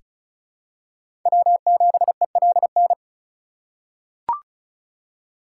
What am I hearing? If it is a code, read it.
Morse code, heard as a keyed tone, about 36 wpm, 700 Hz. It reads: W7ELN
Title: USA Call Signs - 5 Characters - 6x Character Spacing - 35wpm